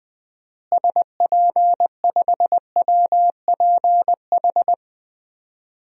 Morse code, heard as a keyed tone, about 20 wpm, 700 Hz. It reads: SP5WPH
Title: DXCC-100 Call Signs - 6 Characters - Repeated 3x - 20wpm